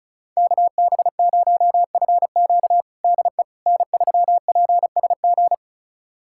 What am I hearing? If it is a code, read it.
Morse code, heard as a keyed tone, about 34 wpm, 690 Hz. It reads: KB0FQ DE N3PSG
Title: USA Callsign DE Callsign - 5 Characters - 35wpm